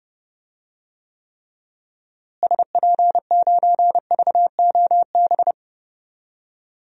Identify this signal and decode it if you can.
Morse code, heard as a keyed tone, about 30 wpm, 700 Hz. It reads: SP9VOB
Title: DXCC-100 Call Signs - 6 Characters - Repeated 2x - 30wpm